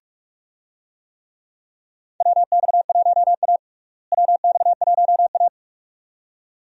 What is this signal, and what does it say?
Morse code, heard as a keyed tone, about 46 wpm, 700 Hz. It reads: WX1A WX1A
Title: USA Call Signs - 4 Characters - Repeated 2x - 45wpm